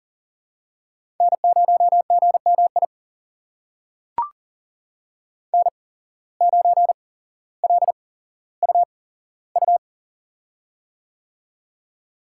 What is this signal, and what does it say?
Morse code, heard as a keyed tone, about 40 wpm, spaced as for 17 wpm, 700 Hz. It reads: N0GMI N9LUU
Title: USA Call Signs - 5 Characters - 8x Character Spacing - 40wpm